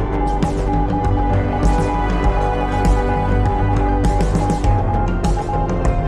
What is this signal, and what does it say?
Morse code, heard as a keyed tone, about 20 wpm, 820 Hz. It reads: NOW.YOU